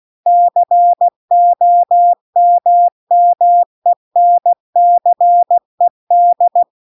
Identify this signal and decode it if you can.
Morse code, heard as a keyed tone, about 16 wpm, 700 Hz. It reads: COMMENCED